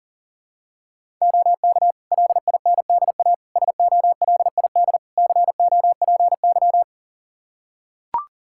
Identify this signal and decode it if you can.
Morse code, heard as a keyed tone, about 40 wpm, 700 Hz. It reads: OK LINDA SOLID COPY
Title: Practice QSO - Exchange 3 - More Info - 40wpm